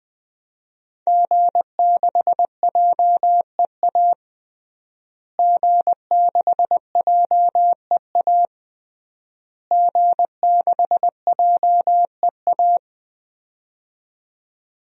Morse code, heard as a keyed tone, about 20 wpm, 700 Hz. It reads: G6JEA G6JEA G6JEA
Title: DXCC-100 Call Signs - 5 Characters - Repeated 3x - 20wpm